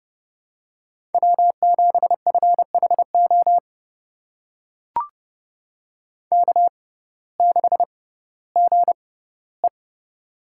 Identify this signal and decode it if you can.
Morse code, heard as a keyed tone, about 30 wpm, spaced as for 17 wpm, 700 Hz. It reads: W7FHO K6GE
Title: USA Call Signs - 6x Character Spacing - 30wpm